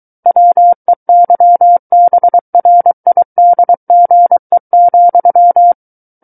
Morse code, heard as a keyed tone, about 23 wpm, 700 Hz. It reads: WEYBRIDGE,
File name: Book One / 14-Chapter 14 - 23 WPM